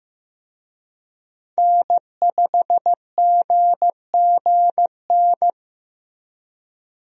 Morse code, heard as a keyed tone, about 15 wpm, 700 Hz. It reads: N5GGN
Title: USA Call Signs - 5 Characters - Repeated 3x - 15wpm